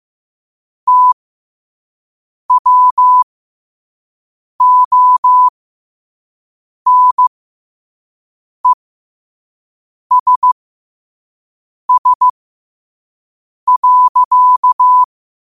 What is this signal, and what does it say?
Morse code, heard as a keyed tone, about 15 wpm, spaced as for 5 wpm, 1000 Hz. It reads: TWONESS.
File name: qotdfiles / 5wpm